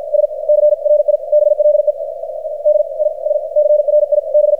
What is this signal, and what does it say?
Morse code, heard as a keyed tone, about 53 wpm, spaced as for 33 wpm, 600 Hz, with radio noise. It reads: SCRIBLED I SCRIB
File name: archived / CRSnet